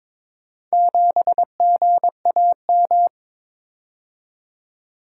Morse code, heard as a keyed tone, about 22 wpm, 700 Hz. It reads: 7GAM
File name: practice / 22wpm / Four Letter-Number Combinations - 22wpm